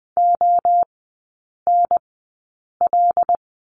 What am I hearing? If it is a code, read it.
Morse code, heard as a keyed tone, about 20 wpm, spaced as for 8 wpm, 700 Hz. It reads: ONL
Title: 074 Lesson 13e - Letter L - Three Characters 20wpm